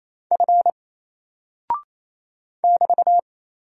Morse code, heard as a keyed tone, about 28 wpm, 700 Hz. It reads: F <BT>
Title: Single Character (Letter:Number:Punctuation:Prosign) Alt 28wpm